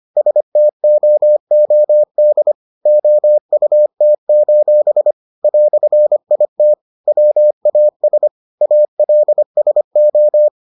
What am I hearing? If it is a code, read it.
Morse code, heard as a keyed tone, about 25 wpm, 600 Hz. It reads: STOOD OUT: "IT WAS ALSO